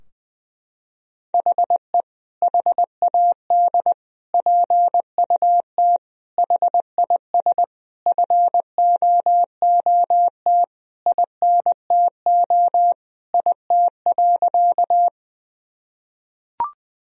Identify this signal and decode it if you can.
Morse code, heard as a keyed tone, about 20 wpm, 700 Hz. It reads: HE HAD PUT HIS FOOT INTO IT.
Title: Sentences from Top 1000 Words - Review Part 3 - 20wpm